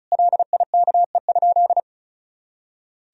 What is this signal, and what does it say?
Morse code, heard as a keyed tone, about 36 wpm, 700 Hz. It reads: LIKE?
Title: Sentences from Top 100 Words and Abbreviations 35wpm